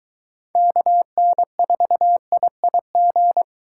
Morse code, heard as a keyed tone, about 22 wpm, 700 Hz. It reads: KN4IIG